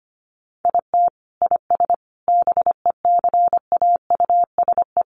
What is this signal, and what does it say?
Morse code, heard as a keyed tone, about 25 wpm, 700 Hz. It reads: IT IS BECAUSE
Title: Sentences from Top 600 Words 25wpm